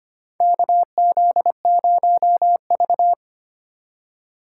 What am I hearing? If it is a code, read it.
Morse code, heard as a keyed tone, about 25 wpm, 700 Hz. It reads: KZ0V
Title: USA Call Signs - 4 Characters - 3x Character Spacing - 25wpm